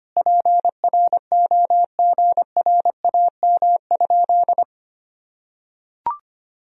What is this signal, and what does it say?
Morse code, heard as a keyed tone, about 25 wpm, 700 Hz. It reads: PROGRAM?
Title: Sentences from Top 1000 Words - Review Part 4 - 25wpm